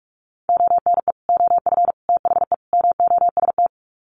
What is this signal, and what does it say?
Morse code, heard as a keyed tone, about 44 wpm, 700 Hz. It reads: ONE OF THE MOST
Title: Sets of 4 Words 45wpm